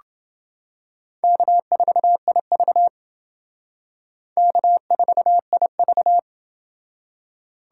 Morse code, heard as a keyed tone, about 28 wpm, 700 Hz. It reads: K4IV K4IV